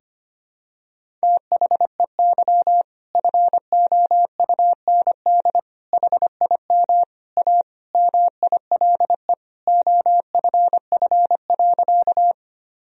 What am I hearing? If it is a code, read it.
Morse code, heard as a keyed tone, about 25 wpm, 700 Hz. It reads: THEY FOUND HIM A MILE OFF.
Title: Sentences from Top 1000 Words - Review Part 1 - 25wpm